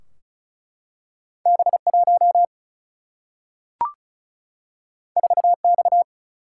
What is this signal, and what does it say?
Morse code, heard as a keyed tone, about 36 wpm, 700 Hz. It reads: B1 4X
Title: Single-Letter Single-Number 35wpm